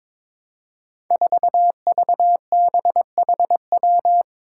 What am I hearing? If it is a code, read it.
Morse code, heard as a keyed tone, about 22 wpm, 700 Hz. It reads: 4VBHW
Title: Five Challenging Letter-Number Combinations 22wpm